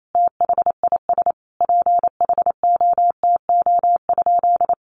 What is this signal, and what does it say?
Morse code, heard as a keyed tone, about 28 wpm, 700 Hz. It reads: THIS PHOTO?
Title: Sentences from Top 1200 Words 28wpm